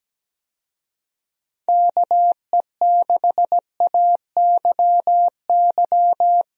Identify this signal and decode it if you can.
Morse code, heard as a keyed tone, about 17 wpm, 700 Hz. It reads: KE6AYY
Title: USA Call Signs - 6 Characters - Repeated 2x - 17wpm